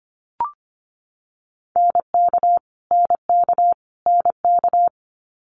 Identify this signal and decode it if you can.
Morse code, heard as a keyed tone, about 25 wpm, 700 Hz. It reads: NK NK NK
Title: Multi-letter Phonemes Repeated 3x 25wpm